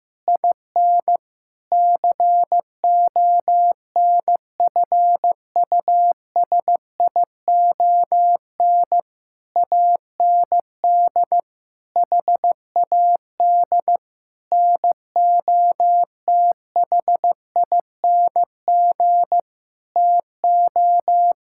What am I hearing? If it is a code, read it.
Morse code, heard as a keyed tone, about 15 wpm, 700 Hz. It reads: IN CONFUSION AND HAD NOTHING TO
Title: Sentences from Top 1600 Words 15wpm